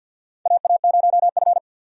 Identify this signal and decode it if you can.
Morse code, heard as a keyed tone, about 52 wpm, 700 Hz. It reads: AA0F
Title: USA Call Signs - 4 Characters - 50wpm